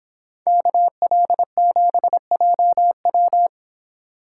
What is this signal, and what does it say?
Morse code, heard as a keyed tone, about 26 wpm, 700 Hz. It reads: KL7JW
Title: Speed Racing - USA 2x2 Call Signs 22wpm